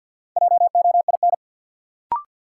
Morse code, heard as a keyed tone, about 47 wpm, 700 Hz. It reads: JOIN
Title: Sentences from Top 1000 Words - Review Part 6 - New Word Drill - 50wpm